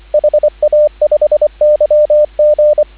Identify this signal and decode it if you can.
Morse code, heard as a keyed tone, about 25 wpm, 600 Hz, with radio noise. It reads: HA5YG
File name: CW / 2017 HA-OB HA5YG